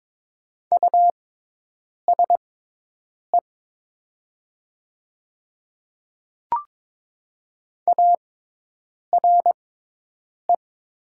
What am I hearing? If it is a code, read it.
Morse code, heard as a keyed tone, about 22 wpm, spaced as for 8 wpm, 700 Hz. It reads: USE ARE
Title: 6x Character Spacing - Top 200 Words - 22wpm